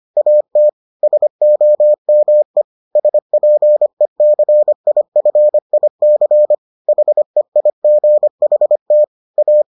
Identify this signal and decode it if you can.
Morse code, heard as a keyed tone, about 25 wpm, 600 Hz. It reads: AT SOME SPECIFIC HEIGHT A